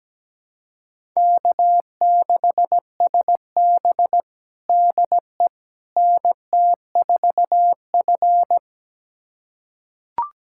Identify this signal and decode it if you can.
Morse code, heard as a keyed tone, about 17 wpm, 700 Hz. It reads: K6SB DE NT4F
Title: USA Callsign DE Callsign - 4 Characters - 17wpm